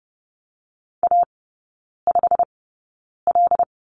Morse code, heard as a keyed tone, about 30 wpm, spaced as for 9 wpm, 700 Hz. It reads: A5L
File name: Lesson 16e - Letter c - Three Characters 30wpm